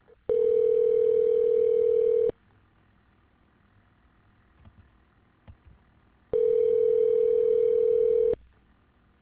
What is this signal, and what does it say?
Ringback tone (North America), 440 + 480 Hz.